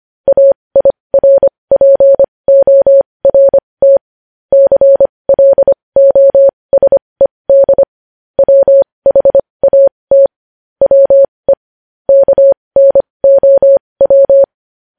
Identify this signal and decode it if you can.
Morse code, heard as a keyed tone, about 25 wpm, spaced as for 20 wpm, 550 Hz. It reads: AIRPORT CLOSED WHAT WE KNOW
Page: news-headlines-in-morse-code-at-20-wpm